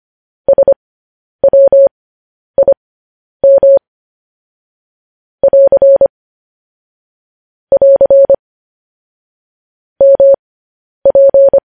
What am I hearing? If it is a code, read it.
Morse code, heard as a keyed tone, about 25 wpm, spaced as for 10 wpm, 550 Hz. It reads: SWIM <AR> <AR> MP
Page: news headlines in morse code at 10 wpm